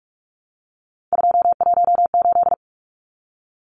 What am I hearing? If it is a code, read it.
Morse code, heard as a keyed tone, about 44 wpm, 700 Hz. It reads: 218
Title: Numbers - 3-Digit 45wpm